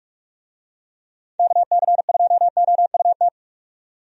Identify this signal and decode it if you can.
Morse code, heard as a keyed tone, about 46 wpm, 700 Hz. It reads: KC2OUT
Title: USA Call Signs - 6 Characters - 45wpm